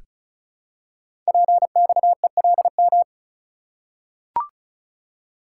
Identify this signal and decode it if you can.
Morse code, heard as a keyed tone, about 36 wpm, 690 Hz. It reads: PXELM
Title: Code Groups - 5 Letters - 35wpm